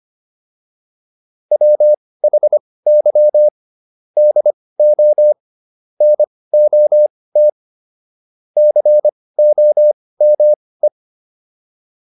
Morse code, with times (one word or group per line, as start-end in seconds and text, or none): 1.5-3.5 WHY
4.2-5.3 DO
6.0-7.5 NOT
8.6-10.9 COME